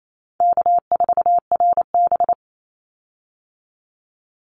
Morse code, heard as a keyed tone, about 28 wpm, 700 Hz. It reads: K4RB
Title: Easy Call Signs - 28wpm